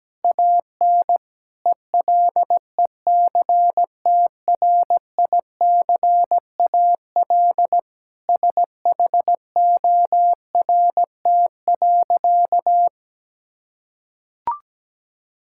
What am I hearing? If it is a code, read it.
Morse code, heard as a keyed tone, about 17 wpm, 700 Hz. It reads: AN ELECTRICAL SHORT.
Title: Sentences from Top 1000 Words - Review Part 6 - 17wpm